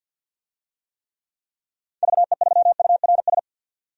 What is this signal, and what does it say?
Morse code, heard as a keyed tone, about 52 wpm, 700 Hz. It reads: VE3URS
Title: DXCC-100 Call Signs - 6 Characters - 50wpm